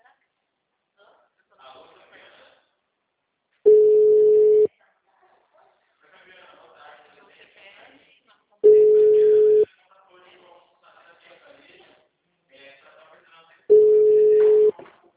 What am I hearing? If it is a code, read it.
Ringback tone (Europe, 425 Hz), with speech.